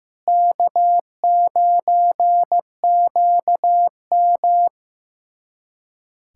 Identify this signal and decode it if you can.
Morse code, heard as a keyed tone, about 15 wpm, 700 Hz. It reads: K9QM
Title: USA Callsign DE Callsign - 4 Characters - 15wpm